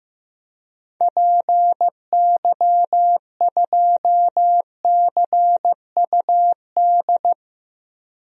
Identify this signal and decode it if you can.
Morse code, heard as a keyed tone, about 15 wpm, 700 Hz. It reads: PY2CUD
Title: DXCC-100 Call Signs - 6 Characters - 15wpm